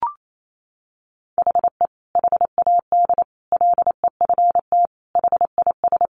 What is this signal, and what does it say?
Morse code, heard as a keyed tone, about 28 wpm, 700 Hz. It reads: HE HAD LEFT HIS
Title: Sentences from Top 1500 Words 28wpm